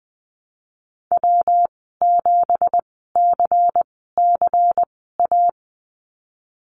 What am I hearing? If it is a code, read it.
Morse code, heard as a keyed tone, about 20 wpm, spaced as for 15 wpm, 700 Hz. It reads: W7CCA
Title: 173 Lesson 28g - Number 9 - Call Signs 20wpm